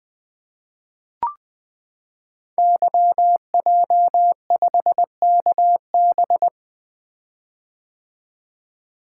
Morse code, heard as a keyed tone, about 20 wpm, 700 Hz. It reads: YJ5KB
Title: Code Groups - 5 Letter:Numbers - 20wpm